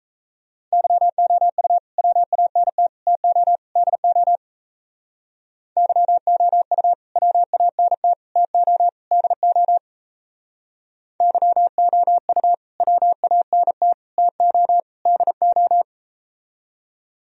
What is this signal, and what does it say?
Morse code, heard as a keyed tone, about 38 wpm, 700 Hz. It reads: YOU WANT TO DO YOU WANT TO DO YOU WANT TO DO